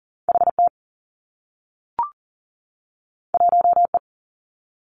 Morse code, heard as a keyed tone, about 38 wpm, 700 Hz. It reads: HT 1E
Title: Two Challenging Letter-Number Combinations 40wpm